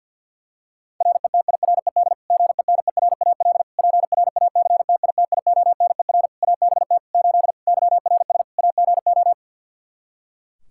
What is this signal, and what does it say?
Morse code, heard as a keyed tone, about 48 wpm, 700 Hz. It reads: RETIRED GENERAL PRACTITIONER ABT 8 YRS AGO
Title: Practice QSO - Exchange 3 - More Info - 50wpm